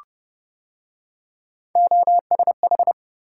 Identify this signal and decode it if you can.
Morse code, heard as a keyed tone, about 29 wpm, 700 Hz. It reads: OSH